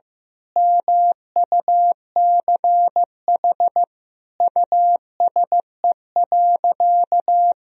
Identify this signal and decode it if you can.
Morse code, heard as a keyed tone, about 15 wpm, 700 Hz. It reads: MUCH USE.
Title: Sentences from Top 1000 Words - Review Part 3 - 15wpm